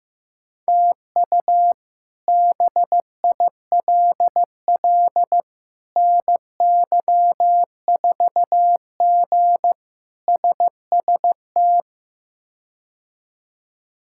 Morse code, heard as a keyed tone, about 16 wpm, 700 Hz. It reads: TU BILL NY4G SST
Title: K1USN SST QSO Practice - 15wpm